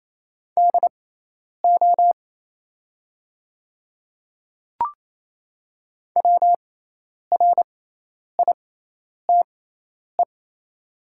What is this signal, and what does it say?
Morse code, heard as a keyed tone, about 28 wpm, spaced as for 10 wpm, 700 Hz. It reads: DO WRITE
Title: 6x Character Spacing - Top 100 Words - 28wpm